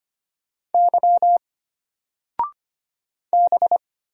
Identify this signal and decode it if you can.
Morse code, heard as a keyed tone, about 25 wpm, 700 Hz. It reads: Y B
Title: Single Letter-Number 25wpm